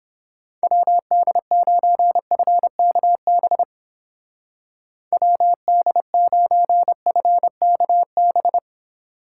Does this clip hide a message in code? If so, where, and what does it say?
Yes: Morse code from 0.6 to 8.6 s, heard as a keyed tone, about 29 wpm, 700 Hz. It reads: WD9FKB WD9FKB